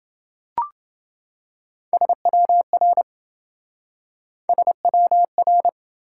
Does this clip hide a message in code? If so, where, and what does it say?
Yes: Morse code from 1.9 to 5.7 s, heard as a keyed tone, about 27 wpm, 700 Hz. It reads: SWR SWR